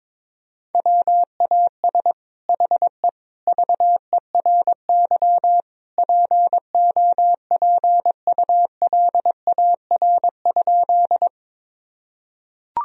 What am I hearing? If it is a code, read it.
Morse code, heard as a keyed tone, about 22 wpm, 700 Hz. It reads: WAS HE VERY POPULAR?
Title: Sentences from Top 1000 Words - Review Part 3 - 22wpm